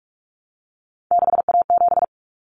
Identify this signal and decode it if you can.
Morse code, heard as a keyed tone, about 44 wpm, 700 Hz. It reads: 6A7
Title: Common Cut Numbers - 3-Digit 45wpm